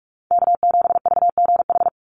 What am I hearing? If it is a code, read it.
Morse code, heard as a keyed tone, about 44 wpm, 700 Hz. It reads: K7VGH
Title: USA Call Signs 45wpm